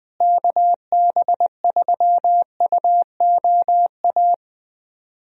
Morse code, heard as a keyed tone, about 20 wpm, 700 Hz. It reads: KB3UOA
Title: USA Call Signs - 6 Characters - Repeated 2x - 20wpm